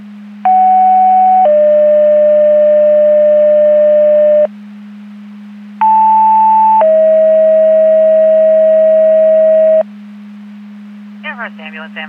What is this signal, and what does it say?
Two-tone page, heard as a one-second tone, then a three-second tone, about 750 then 600 Hz, about 880 then 640 Hz. It is heard from 0.5 to 9.8 s.
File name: Portage 154.385 Paging